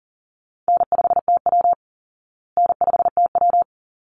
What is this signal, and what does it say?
Morse code, heard as a keyed tone, about 40 wpm, 700 Hz. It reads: N5TW N5TW
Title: Easy Call Signs Repeated 2x - 40wpm